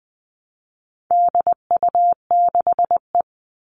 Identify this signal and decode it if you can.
Morse code, heard as a keyed tone, about 20 wpm, 700 Hz. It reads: DU6E
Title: All Cut Numbers - 4-Digit 20wpm